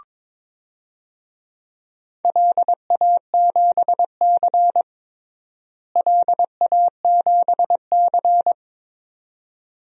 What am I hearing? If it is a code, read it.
Morse code, heard as a keyed tone, about 22 wpm, 700 Hz. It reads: LA7C LA7C
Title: DXCC-100 Call Signs - 4 Characters - Repeated 2x - 22wpm